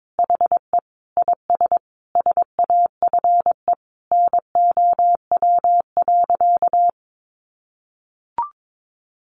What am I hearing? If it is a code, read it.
Morse code, heard as a keyed tone, about 22 wpm, 700 Hz. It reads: HE IS SAFE NOW.
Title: Sentences from Top 900 Words 22wpm